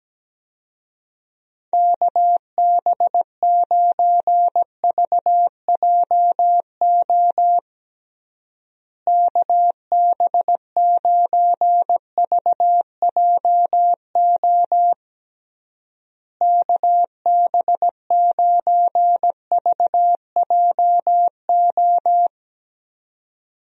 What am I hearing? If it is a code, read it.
Morse code, heard as a keyed tone, about 17 wpm, 700 Hz. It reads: KB9VJO KB9VJO KB9VJO
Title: USA Call Signs - 6 Characters - Repeated 3x - 17wpm